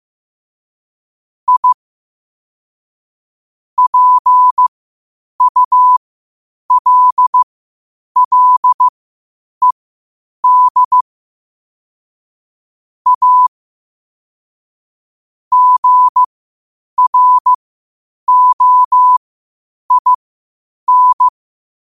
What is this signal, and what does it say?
Morse code, heard as a keyed tone, about 15 wpm, spaced as for 8 wpm, 1000 Hz. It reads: I PULLED A GROIN
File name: qotdfiles / 8wpm